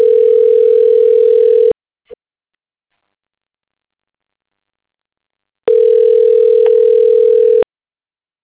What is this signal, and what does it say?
Ringback tone (North America), 440 + 480 Hz.